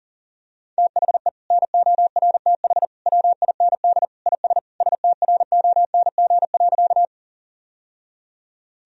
Morse code, heard as a keyed tone, about 40 wpm, 700 Hz. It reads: THE NORTH WIND IS STRONG.
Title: Sentences from Top 1000 Words - Review Part 2 - 40wpm